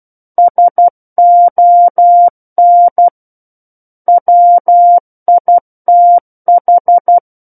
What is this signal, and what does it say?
Morse code, heard as a keyed tone, about 12 wpm, 700 Hz. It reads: SON WITH
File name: Book One / 03-Chapter 3 - 12 WPM